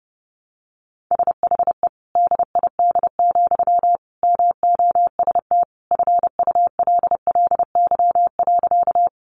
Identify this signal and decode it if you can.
Morse code, heard as a keyed tone, about 28 wpm, 700 Hz. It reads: SHE DID, MOST FULLY.
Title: Sentences from Top 1300 Words 30wpm